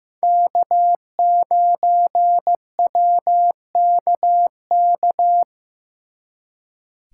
Morse code, heard as a keyed tone, about 18 wpm, 700 Hz. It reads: K9WKK
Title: Speed Racing - USA Call Signs 15wpm